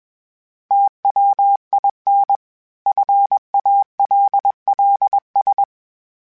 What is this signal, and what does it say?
Morse code, heard as a keyed tone, about 21 wpm, 800 Hz. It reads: TWIN FALLS
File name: fromSMs / FromSMs mp3s / Cities@20